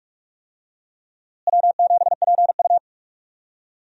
Morse code, heard as a keyed tone, about 41 wpm, 700 Hz. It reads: W7PU
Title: USA Call Signs - 4 Characters - Repeated 2x - 45wpm